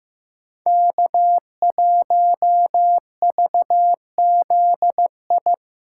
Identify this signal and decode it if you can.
Morse code, heard as a keyed tone, about 18 wpm, 700 Hz. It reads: K1VZI
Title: Speed Racing - USA 1x3 Call Signs 15wpm